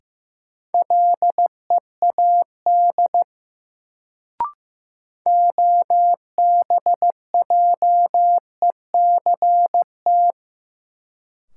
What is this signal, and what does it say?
Morse code, heard as a keyed tone, about 15 wpm, 700 Hz. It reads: LEAD OBJECT
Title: Sentences from Top 500 Words - New Word Drill - 15wpm